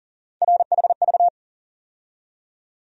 Morse code, heard as a keyed tone, about 37 wpm, 700 Hz. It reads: RHV
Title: USA Call Sign Postfixes - 3 Letters 40wpm